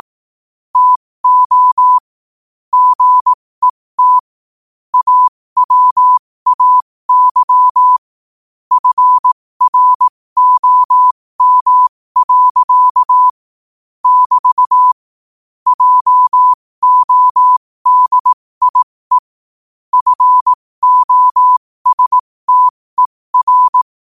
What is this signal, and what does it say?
Morse code, heard as a keyed tone, about 18 wpm, spaced as for 15 wpm, 1000 Hz. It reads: TO GET AWAY FROM. <BT> JODIE FOSTER
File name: qotdfiles / 15wpm